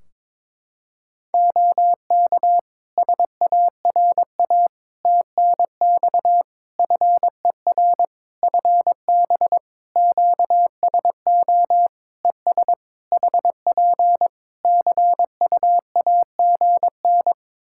Morse code, heard as a keyed tone, about 22 wpm, 700 Hz. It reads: OK SARA TNX FER FB QSO ES HP CUAGN
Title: Practice QSO - Exchange 4 - Ending - 22wpm